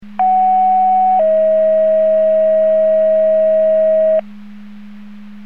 Two-tone page, heard as a one-second tone, then a three-second tone, about 750 then 640 Hz. It is heard from 0.2 to 4.2 s.
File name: Portage 154.385 Paging